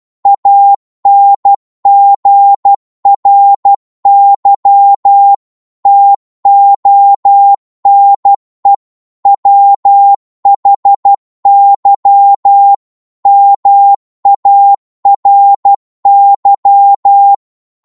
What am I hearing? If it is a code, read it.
Morse code, heard as a keyed tone, about 12 wpm, 800 Hz. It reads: ANGRY TONE WHY MARY